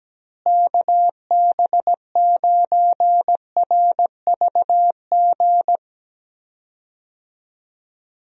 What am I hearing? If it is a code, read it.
Morse code, heard as a keyed tone, about 17 wpm, 700 Hz. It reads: KB9RVG
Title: USA Call Signs - 6 Characters - Repeated 2x - 17wpm